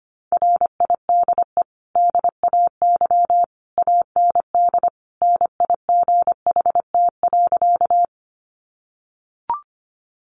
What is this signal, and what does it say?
Morse code, heard as a keyed tone, about 25 wpm, 700 Hz. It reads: RIDE DAY AND NIGHT.
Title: Sentences from Top 1200 Words 25wpm